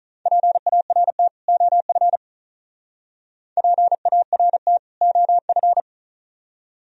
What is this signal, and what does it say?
Morse code, heard as a keyed tone, about 42 wpm, 700 Hz. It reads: PART OF PART OF
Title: Speed-Racing - Sets of 2 Words 35wpm